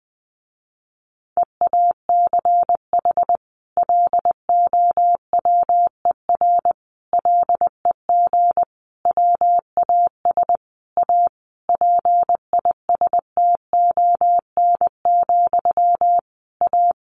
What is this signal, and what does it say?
Morse code, heard as a keyed tone, about 20 wpm, 700 Hz. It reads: EACH LOWER LEG WAS A PISTON, A
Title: A Sound Of Thunder 20wpm